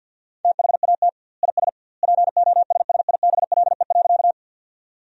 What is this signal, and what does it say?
Morse code, heard as a keyed tone, about 50 wpm, 700 Hz. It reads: THAT IS POSSIBLE.
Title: 2x Word-Spacing - Sentences from Top 500 Words - 50wpm